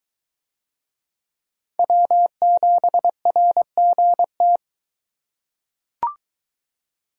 Morse code, heard as a keyed tone, about 22 wpm, 700 Hz. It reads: W7RGT